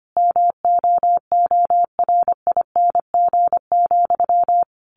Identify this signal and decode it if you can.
Morse code, heard as a keyed tone, about 25 wpm, 700 Hz. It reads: MOORING,
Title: A Sound Of Thunder 25wpm